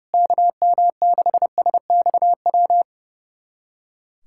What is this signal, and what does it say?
Morse code, heard as a keyed tone, about 30 wpm, 700 Hz. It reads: KM6SXW
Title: USA Callsign DE Callsign 30wpm